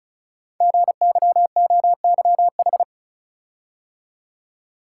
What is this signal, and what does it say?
Morse code, heard as a keyed tone, about 36 wpm, 690 Hz. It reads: GYOYH
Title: Code Groups - 5 Letter:Numbers - 35wpm